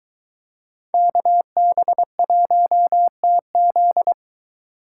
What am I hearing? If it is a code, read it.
Morse code, heard as a keyed tone, about 23 wpm, 700 Hz. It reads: KB1TZ